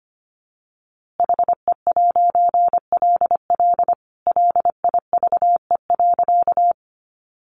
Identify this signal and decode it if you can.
Morse code, heard as a keyed tone, about 25 wpm, 700 Hz. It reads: HE'LL LIVE.
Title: A Sound Of Thunder 25wpm